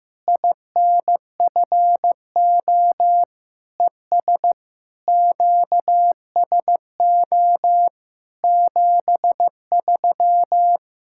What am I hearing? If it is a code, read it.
Morse code, heard as a keyed tone, about 15 wpm, 700 Hz. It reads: INFO ES QSO 73
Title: Practice QSO - Exchange 4 - Ending - 15wpm